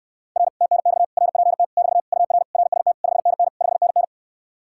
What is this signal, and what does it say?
Morse code, heard as a keyed tone, about 69 wpm, 700 Hz. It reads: R TNX RPT <BT> UR RST 5NN 5NN